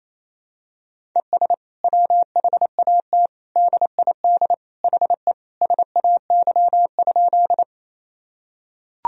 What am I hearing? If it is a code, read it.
Morse code, heard as a keyed tone, about 28 wpm, 700 Hz. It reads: ES WHAT DID HE SAY?
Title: Sentences from Top 200 Words and Abbreviations 28wpm